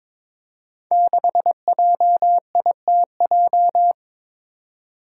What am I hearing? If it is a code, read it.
Morse code, heard as a keyed tone, about 21 wpm, 700 Hz. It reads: 6JITJ
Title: Five Challenging Letter-Number Combinations 22wpm